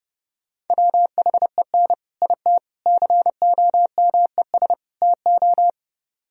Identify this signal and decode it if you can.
Morse code, heard as a keyed tone, about 37 wpm, 700 Hz. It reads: WHEN IT COMES TO
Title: Speed-Racing - Sets of 4 Words 30wpm